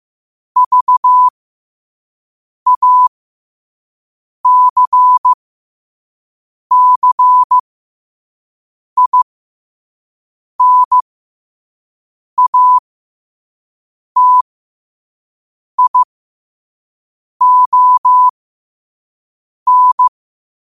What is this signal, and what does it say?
Morse code, heard as a keyed tone, about 15 wpm, spaced as for 5 wpm, 1000 Hz. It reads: VACCINATION
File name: qotdfiles / 5wpm